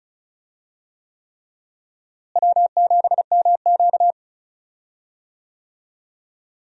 Morse code, heard as a keyed tone, about 36 wpm, 700 Hz. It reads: W7MQ
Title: USA Call Signs - 4 Characters - Repeated 3x - 35wpm